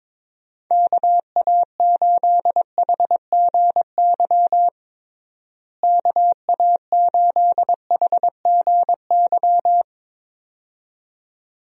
Morse code, heard as a keyed tone, about 22 wpm, 700 Hz. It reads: KA8HGY KA8HGY
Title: USA Call Signs - 6 Characters - Repeated 2x - 22wpm